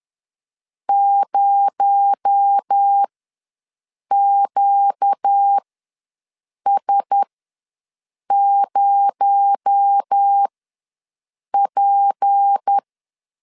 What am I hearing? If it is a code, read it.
Morse code, heard as a keyed tone, about 10 wpm, spaced as for 6 wpm, 800 Hz. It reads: 0QS0P
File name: LEZIONI / 6 PKX0 / 6 PKX0 ws 10wpm 4220